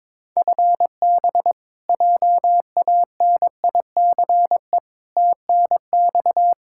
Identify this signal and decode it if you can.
Morse code, heard as a keyed tone, about 22 wpm, 700 Hz. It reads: FB JANICE TNX
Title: K1USN SST QSO Practice 22wpm